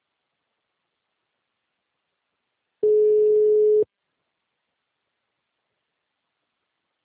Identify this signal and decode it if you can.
Ringback tone (Europe, 425 Hz).
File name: out- -2008- - - .19